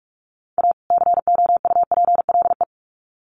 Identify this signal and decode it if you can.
Morse code, heard as a keyed tone, about 44 wpm, 700 Hz. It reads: A COUPLE
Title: Sets of 2 Words 45wpm